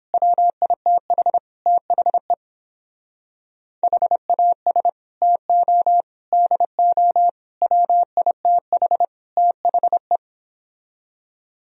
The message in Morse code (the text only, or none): WITH THE HAS TO DO WITH THE